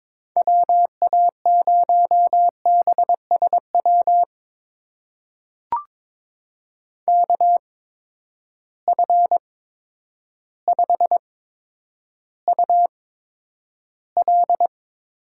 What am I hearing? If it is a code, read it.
Morse code, heard as a keyed tone, about 22 wpm, spaced as for 10 wpm, 700 Hz. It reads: WA0BSW KF5UL
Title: USA Call Signs - 6 Characters - 8x Character Spacing - 22wpm